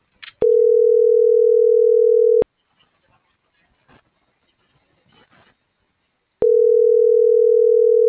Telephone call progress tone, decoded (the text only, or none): ringback tone